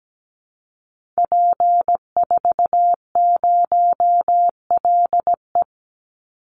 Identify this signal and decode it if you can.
Morse code, heard as a keyed tone, about 17 wpm, 700 Hz. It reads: P40LE
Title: Call Signs Repeated 2x - 17wpm